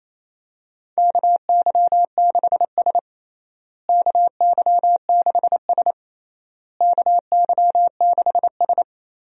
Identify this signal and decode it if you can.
Morse code, heard as a keyed tone, about 27 wpm, 700 Hz. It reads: KY6S KY6S KY6S
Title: USA Call Signs - 4 Characters - Repeated 3x - 28wpm